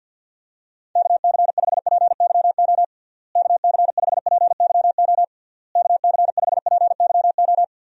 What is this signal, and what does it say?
Morse code, heard as a keyed tone, about 49 wpm, 700 Hz. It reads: KC5PYO KC5PYO KC5PYO
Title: USA Call Signs - 6 Characters - Repeated 3x - 50wpm